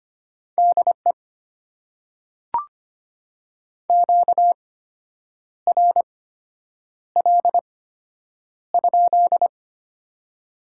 Morse code, heard as a keyed tone, about 25 wpm, spaced as for 8 wpm, 690 Hz. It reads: DE QRL?
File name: Top 90 QSO Elements - 8x Character Spacing - 25wpm